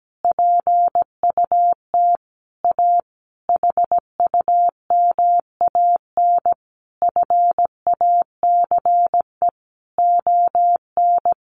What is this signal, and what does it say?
Morse code, heard as a keyed tone, about 17 wpm, 700 Hz. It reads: PUT A HUMAN FACE ON
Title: Sentences from Top 1200 Words 17wpm